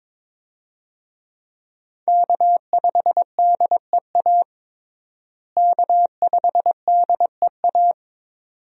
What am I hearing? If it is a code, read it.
Morse code, heard as a keyed tone, about 22 wpm, 700 Hz. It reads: K5DEA K5DEA
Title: USA Call Signs - 5 Characters - Repeated 2x - 22wpm